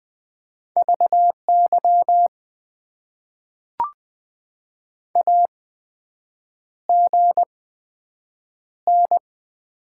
Morse code, heard as a keyed tone, about 20 wpm, spaced as for 7 wpm, 700 Hz. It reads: VY AGN
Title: Top 90 QSO Elements - 8x Character Spacing - 20wpm